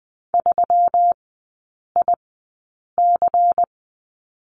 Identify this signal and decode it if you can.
Morse code, heard as a keyed tone, about 20 wpm, spaced as for 8 wpm, 700 Hz. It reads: 3IC
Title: 166 Lesson 27e - Number 7 - Three Characters 20wpm